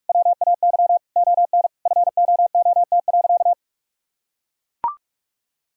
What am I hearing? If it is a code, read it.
Morse code, heard as a keyed tone, about 43 wpm, 700 Hz. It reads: WAY ON FOOT.
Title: Sentences from Top 1000 Words - Review Part 3 - 45wpm